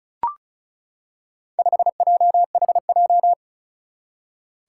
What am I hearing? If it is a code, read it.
Morse code, heard as a keyed tone, about 36 wpm, 700 Hz. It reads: 5JHJ
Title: Four Challenging Letter-Number Combinations 35wpm